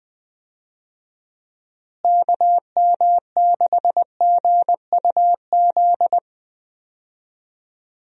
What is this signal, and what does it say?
Morse code, heard as a keyed tone, about 20 wpm, 700 Hz. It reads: KM6GUZ
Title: USA Call Signs - 6 Characters - Repeated 2x - 20wpm